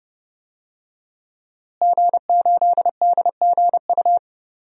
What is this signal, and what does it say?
Morse code, heard as a keyed tone, about 30 wpm, 700 Hz. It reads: G8DGU
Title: DXCC-100 Call Signs - 5 Characters - Repeated 3x - 30wpm